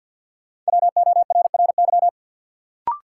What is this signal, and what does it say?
Morse code, heard as a keyed tone, about 52 wpm, 700 Hz. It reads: WORRY
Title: Sentences from Top 1100 Words - New Word Drill Alt - 50wpm